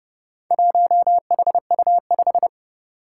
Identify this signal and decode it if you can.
Morse code, heard as a keyed tone, about 30 wpm, 690 Hz. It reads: 1HU5
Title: Four Challenging Letter-Number Combinations 30wpm